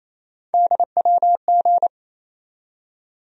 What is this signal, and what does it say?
Morse code, heard as a keyed tone, about 28 wpm, 700 Hz. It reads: DWG
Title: USA Call Sign Postfixes - 3 Letters 28wpm